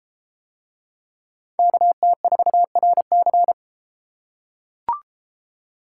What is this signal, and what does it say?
Morse code, heard as a keyed tone, about 31 wpm, 700 Hz. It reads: KT4RC